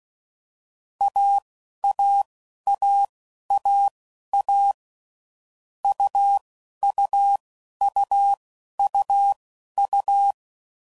Morse code, heard as a keyed tone, about 16 wpm, spaced as for 11 wpm, 800 Hz. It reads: AAAAA UUUUU
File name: Combinations / Difficult Combos / AUV AUV4